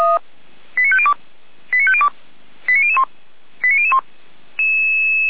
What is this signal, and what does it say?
Five-tone selective call (ZVEI), heard as quick runs of five tones, tones of about 70 ms. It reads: 89502 89502 89001 89001